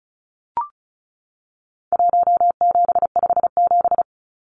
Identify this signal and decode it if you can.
Morse code, heard as a keyed tone, about 34 wpm, 700 Hz. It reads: 1757
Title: Numbers - Contest Style - 4-Digit 35wpm